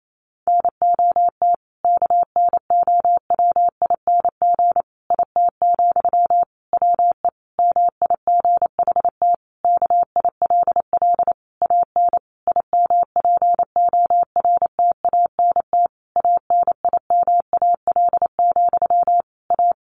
Morse code, heard as a keyed tone, about 28 wpm, 700 Hz. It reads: NOT KNOWING IT, WE MIGHT KILL AN IMPORTANT ANIMAL, A
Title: A Sound Of Thunder 28wpm